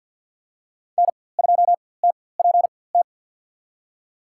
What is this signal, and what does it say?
Morse code, heard as a keyed tone, about 53 wpm, spaced as for 24 wpm, 700 Hz. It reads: N2TPT
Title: USA Call Signs - 4x Character Spacing - 50wpm